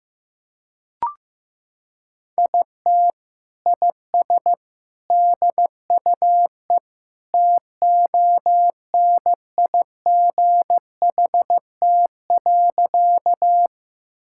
Morse code, heard as a keyed tone, about 16 wpm, 700 Hz. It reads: IT IS DUE TONIGHT.
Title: Sentences from Top 1000 Words - Review Part 6 - 15wpm